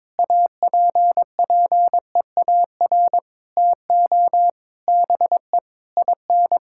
Morse code, heard as a keyed tone, about 21 wpm, 700 Hz. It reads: APPEAR TO BE IN
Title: Sentences from Top 1000 Words - Review Part 6 - 22wpm